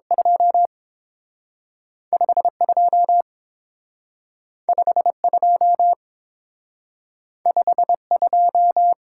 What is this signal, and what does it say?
Morse code, heard as a keyed tone, about 30 wpm, spaced as for 6 wpm, 700 Hz. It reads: 25 25 25 2